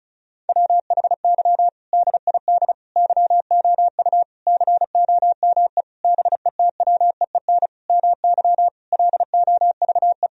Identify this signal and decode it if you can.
Morse code, heard as a keyed tone, about 36 wpm, 700 Hz. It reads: WHY DID YOU COME BETWEEN MY LOVE
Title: Sentences from Top 1000 Words - Review Part 1 - 35wpm